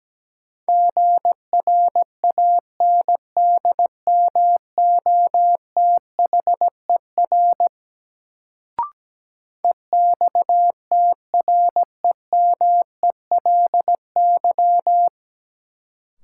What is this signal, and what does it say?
Morse code, heard as a keyed tone, about 17 wpm, 700 Hz. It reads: GRANDMOTHER EXTREMELY
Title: Sentences from Top 1200 Words - New Word Drill Alt - 17wpm